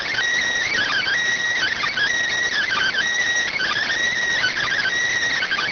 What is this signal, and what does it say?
Emergency Alert System (SAME) tones (header bursts), heard as harsh buzzing bursts. It is heard from 0.7 to 4.9 s.
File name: findings / S ss-portrait.gen-19 rx